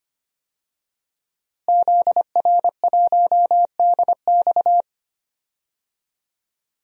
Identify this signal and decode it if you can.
Morse code, heard as a keyed tone, about 25 wpm, 700 Hz. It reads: ZR1DX
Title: DXCC-100 Call Signs - 5 Characters - Repeated 3x - 25wpm